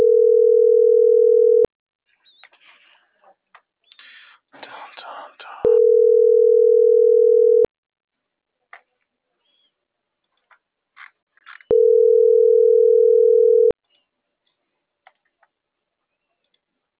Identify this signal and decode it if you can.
Ringback tone (North America), 440 + 480 Hz.